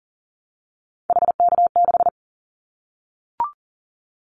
Morse code, heard as a keyed tone, about 40 wpm, 700 Hz. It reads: HK6
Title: Callsign Prefixes for Top 100 DXCC - 40wpm